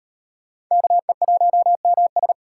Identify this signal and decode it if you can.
Morse code, heard as a keyed tone, about 38 wpm, 700 Hz. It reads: KE1MS